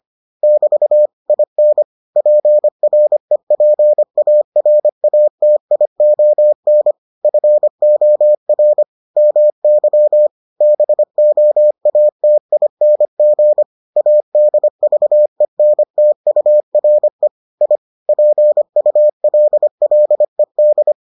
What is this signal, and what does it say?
Morse code, heard as a keyed tone, about 25 wpm, 600 Hz. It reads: <BT> IN PREPARATION FOR MY BOATING ADVENTURE I PULLED